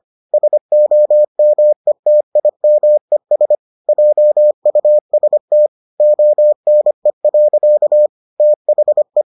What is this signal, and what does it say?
Morse code, heard as a keyed tone, about 25 wpm, 600 Hz. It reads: SOMETIMES JUST ONE. THE